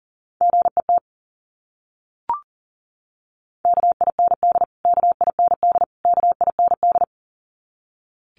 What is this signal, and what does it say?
Morse code, heard as a keyed tone, about 40 wpm, 700 Hz. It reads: GET KIND KIND KIND
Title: Top 500 Words Repeated 40wpm